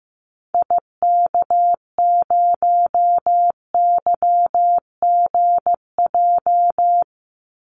Morse code, heard as a keyed tone, about 15 wpm, 700 Hz. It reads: IK0YGJ
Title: Call Signs Repeated 2x - 15wpm